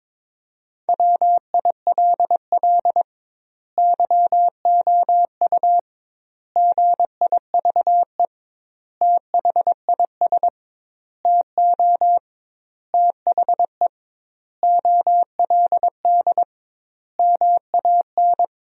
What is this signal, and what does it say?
Morse code, heard as a keyed tone, about 22 wpm, 700 Hz. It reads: WILL YOU GIVE THIS TO THE OLD MAN
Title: 2x Word-Spacing - Sentences from Top 200 Words - 22wpm